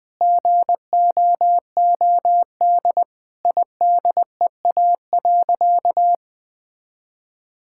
Morse code, heard as a keyed tone, about 21 wpm, 700 Hz. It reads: GOOD IDEA.
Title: Sentences with 4-Word Phrases - 20wpm